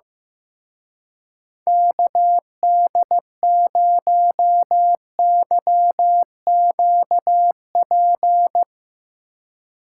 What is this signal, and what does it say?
Morse code, heard as a keyed tone, about 15 wpm, 700 Hz. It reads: KD0YQP
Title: Speed Racing - USA 2x3 Call Signs 15wpm